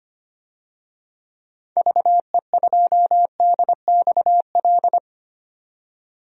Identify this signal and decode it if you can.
Morse code, heard as a keyed tone, about 25 wpm, 700 Hz. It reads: VE2DXL
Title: DXCC-100 Call Signs - 6 Characters - 25wpm